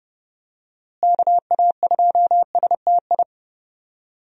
Morse code, heard as a keyed tone, about 32 wpm, 700 Hz. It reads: KA2STI
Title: USA Call Signs - 6 Characters - Repeated 2x - 30wpm